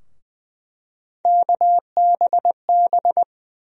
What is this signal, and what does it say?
Morse code, heard as a keyed tone, about 20 wpm, 700 Hz. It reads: KBB
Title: USA Call Sign Postfixes - 3 Letters 20wpm